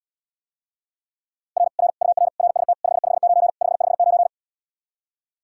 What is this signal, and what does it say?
Morse code, heard as a keyed tone, about 80 wpm, 700 Hz. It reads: R R UR RST 559 559